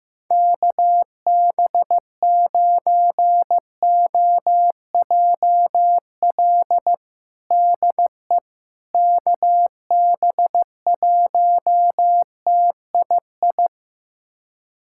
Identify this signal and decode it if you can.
Morse code, heard as a keyed tone, about 15 wpm, 700 Hz. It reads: KB9OJL DE KB1TII
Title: USA Callsign DE Callsign 15wpm